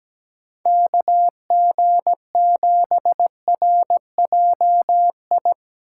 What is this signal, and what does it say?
Morse code, heard as a keyed tone, about 19 wpm, 700 Hz. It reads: KG7RJI
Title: Speed Racing - USA 2x3 Call Signs 17wpm